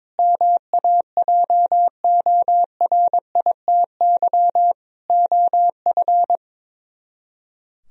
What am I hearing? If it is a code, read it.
Morse code, heard as a keyed tone, about 22 wpm, 700 Hz. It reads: MAJORITY OF
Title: Speed-Racing - Sets of 4 Words 22wpm